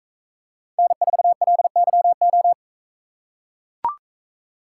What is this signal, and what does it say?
Morse code, heard as a keyed tone, about 42 wpm, 700 Hz. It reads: N4LYO